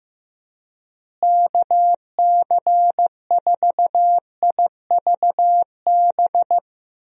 Morse code, heard as a keyed tone, about 15 wpm, 700 Hz. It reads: KC4IVB
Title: USA Call Signs - 6 Characters - Repeated 3x - 15wpm